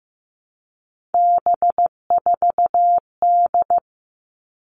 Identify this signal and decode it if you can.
Morse code, heard as a keyed tone, about 15 wpm, 700 Hz. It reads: B4D
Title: All Cut Numbers - 3-Digit 15wpm